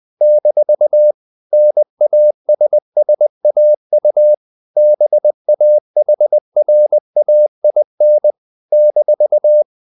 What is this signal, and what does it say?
Morse code, heard as a keyed tone, about 20 wpm, 600 Hz. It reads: - NASSAU BAHRAIN -
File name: CW / Practice / NationalCapitals20